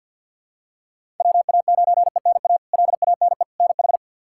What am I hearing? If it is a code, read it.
Morse code, heard as a keyed tone, about 48 wpm, 700 Hz. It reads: WA9ENA LANE NH
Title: K1USN SST QSO Practice - 50wpm